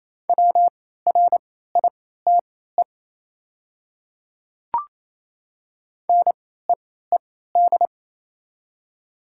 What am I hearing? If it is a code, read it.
Morse code, heard as a keyed tone, about 27 wpm, spaced as for 16 wpm, 700 Hz. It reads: WRITE NEED
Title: 3x Character Spacing - Top 200 Words - 28wpm